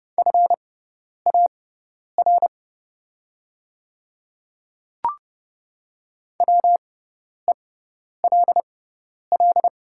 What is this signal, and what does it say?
Morse code, heard as a keyed tone, about 31 wpm, spaced as for 10 wpm, 700 Hz. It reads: FAR WELL
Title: 6x Character Spacing - Top 300 Words - 30wpm